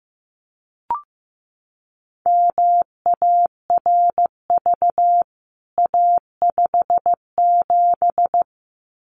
Morse code, heard as a keyed tone, about 15 wpm, 700 Hz. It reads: MARV A57
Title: CWT Exchange 15wpm